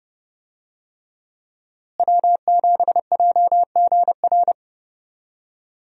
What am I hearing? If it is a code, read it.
Morse code, heard as a keyed tone, about 30 wpm, 700 Hz. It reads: W7JGR
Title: USA Call Signs - 5 Characters - Repeated 3x - 30wpm